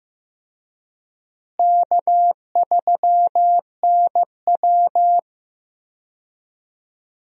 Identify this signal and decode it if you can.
Morse code, heard as a keyed tone, about 15 wpm, 700 Hz. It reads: K3NW
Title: USA Call Signs - 4 Characters - Repeated 2x - 15wpm